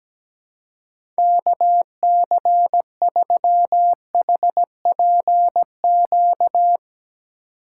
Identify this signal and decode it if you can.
Morse code, heard as a keyed tone, about 17 wpm, 700 Hz. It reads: KC3HPQ
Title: USA Call Signs - 6 Characters - 2x Character Spacing - 17wpm